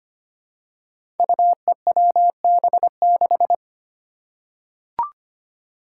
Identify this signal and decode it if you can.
Morse code, heard as a keyed tone, about 25 wpm, 700 Hz. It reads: UEWB6
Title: Five Challenging Letter-Number Combinations 25wpm